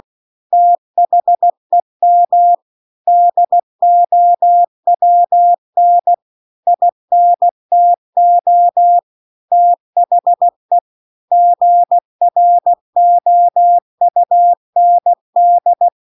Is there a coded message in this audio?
Yes: Morse code from 0.5 to 15.9 s, heard as a keyed tone, about 16 wpm, 700 Hz. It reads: THEM DOWN INTO THE GROUND